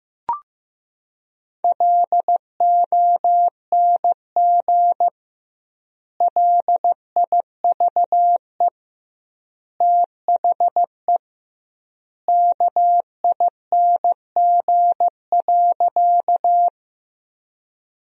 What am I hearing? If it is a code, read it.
Morse code, heard as a keyed tone, about 15 wpm, 700 Hz. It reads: LONG LIVE THE KING.
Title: 2x Word-Spacing - Sentences from Top 400 Words - 15wpm